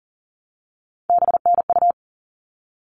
Morse code, heard as a keyed tone, about 40 wpm, 700 Hz. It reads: BNU
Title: All Cut Numbers - 3-Digit 40wpm